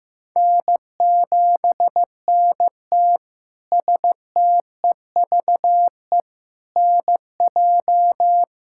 Morse code, heard as a keyed tone, about 15 wpm, 700 Hz. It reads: N7NT STEVE NJ
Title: K1USN SST QSO Practice - 15wpm